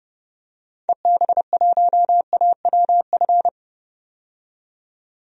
Morse code, heard as a keyed tone, about 32 wpm, 700 Hz. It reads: EB1AWF
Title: DXCC-100 Call Signs - 6 Characters - 30wpm